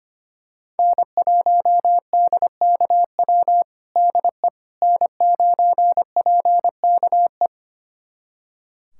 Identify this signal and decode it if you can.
Morse code, heard as a keyed tone, about 25 wpm, 700 Hz. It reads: N1DKW DE N9PKE
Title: USA Callsign DE Callsign - 5 Characters - 25wpm